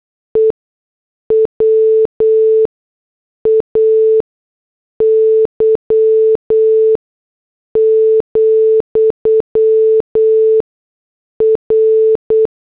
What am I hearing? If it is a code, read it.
Morse code, heard as a keyed tone, about 8 wpm, spaced as for 6 wpm, 440 Hz. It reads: E WAY, R
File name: morse / headlines